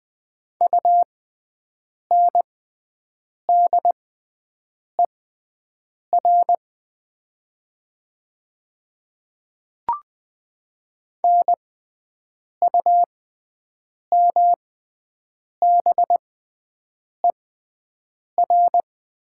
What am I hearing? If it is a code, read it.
Morse code, heard as a keyed tone, about 20 wpm, spaced as for 7 wpm, 700 Hz. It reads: UNDER NUMBER
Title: 6x Character Spacing - Top 200 Words - 20wpm